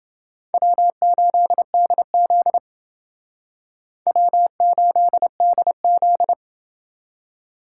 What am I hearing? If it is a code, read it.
Morse code, heard as a keyed tone, about 28 wpm, 700 Hz. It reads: W8DZ W8DZ